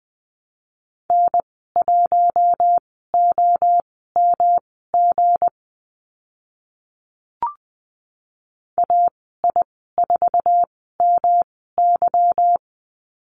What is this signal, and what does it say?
Morse code, heard as a keyed tone, about 20 wpm, spaced as for 14 wpm, 700 Hz. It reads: N1OMG AI4MY
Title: 161 Lesson 26g - Letter g - Call Signs 20wpm